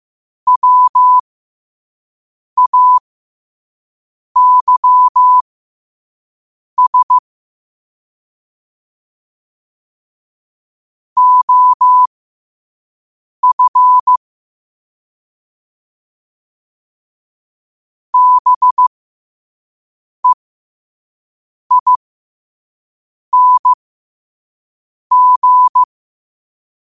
Morse code, heard as a keyed tone, about 15 wpm, spaced as for 5 wpm, 1000 Hz. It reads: WAYS OF BEING
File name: qotdfiles / 5wpm